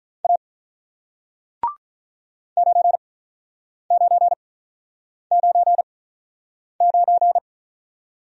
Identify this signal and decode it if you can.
Morse code, heard as a keyed tone, about 47 wpm, spaced as for 9 wpm, 700 Hz. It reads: A 9999